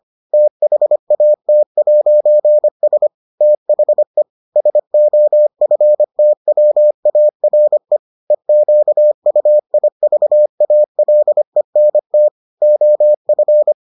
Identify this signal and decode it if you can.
Morse code, heard as a keyed tone, about 25 wpm, 600 Hz. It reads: THAT'S THE SOFTWARE EQUIVALENT OF